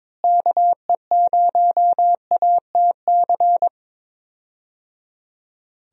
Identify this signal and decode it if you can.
Morse code, heard as a keyed tone, about 22 wpm, 700 Hz. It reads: KE0ATC
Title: USA Call Signs - 6 Characters - Repeated 3x - 22wpm